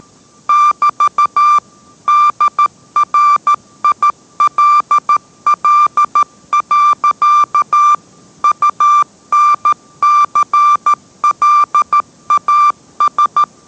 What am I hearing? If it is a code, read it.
Morse code, heard as a keyed tone, about 13 wpm, 1180 Hz. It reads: <BT> DRILL. UNCLAS